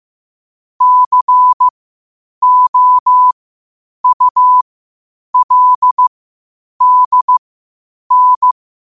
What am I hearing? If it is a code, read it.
Morse code, heard as a keyed tone, about 15 wpm, spaced as for 8 wpm, 1000 Hz. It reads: COULDN
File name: qotdfiles / 8wpm